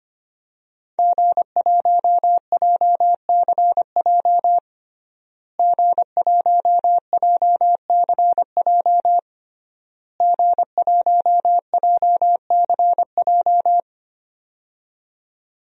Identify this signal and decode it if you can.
Morse code, heard as a keyed tone, about 25 wpm, 700 Hz. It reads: G1JCJ G1JCJ G1JCJ
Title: DXCC-100 Call Signs - 5 Characters - Repeated 3x - 25wpm